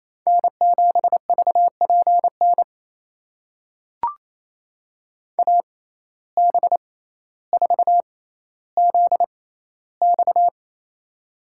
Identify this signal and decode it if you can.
Morse code, heard as a keyed tone, about 28 wpm, spaced as for 14 wpm, 700 Hz. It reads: N7VPN AB4ZX
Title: USA Call Signs - 5 Characters - 6x Character Spacing - 28wpm